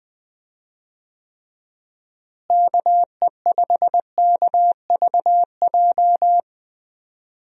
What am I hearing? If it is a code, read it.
Morse code, heard as a keyed tone, about 20 wpm, 700 Hz. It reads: KE5KVJ
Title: USA Call Signs - 6 Characters - Repeated 3x - 20wpm